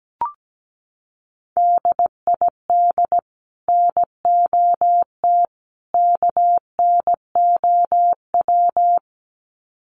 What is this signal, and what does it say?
Morse code, heard as a keyed tone, about 17 wpm, 700 Hz. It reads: DID NOT KNOW
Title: Sets of 3 Words 17wpm